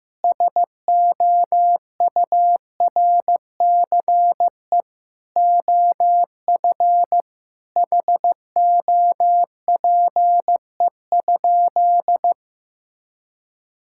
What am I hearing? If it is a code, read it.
Morse code, heard as a keyed tone, about 15 wpm, 700 Hz. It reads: SOURCE OF HOPE?
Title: Sentences from Top 1000 Words - Review Part 5 - 15wpm